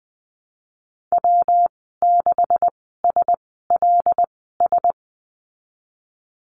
Morse code, heard as a keyed tone, about 20 wpm, spaced as for 15 wpm, 700 Hz. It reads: W6SLS
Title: 147 Lesson 24g - Letter y - Call Signs 20wpm